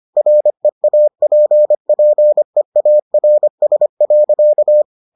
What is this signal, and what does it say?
Morse code, heard as a keyed tone, about 25 wpm, 600 Hz. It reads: REAPPEARS.